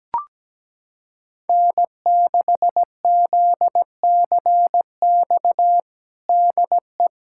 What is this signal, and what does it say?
Morse code, heard as a keyed tone, about 17 wpm, 700 Hz. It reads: N6ZCX DE
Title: USA Callsign DE Callsign - 5 Characters - 17wpm